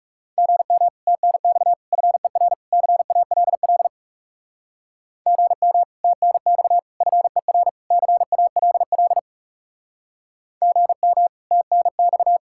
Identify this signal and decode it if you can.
Morse code, heard as a keyed tone, about 39 wpm, 700 Hz. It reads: GM TNX FER CALL GM TNX FER CALL GM TNX